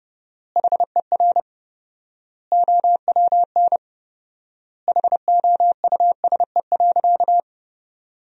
Morse code, heard as a keyed tone, about 30 wpm, 700 Hz. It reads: HER OWN HOUSE.
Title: 4x Word-Spacing - Sentences from Top 300 Words - 30wpm